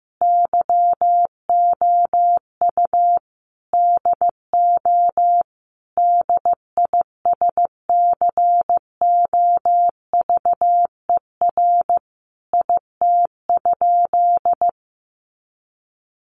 Morse code, heard as a keyed tone, about 15 wpm, 700 Hz. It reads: YOU DO DISCOVER IT?
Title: Sentences from Top 1100 Words 15wpm